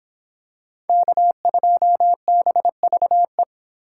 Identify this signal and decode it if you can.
Morse code, heard as a keyed tone, about 26 wpm, 700 Hz. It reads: K2BVE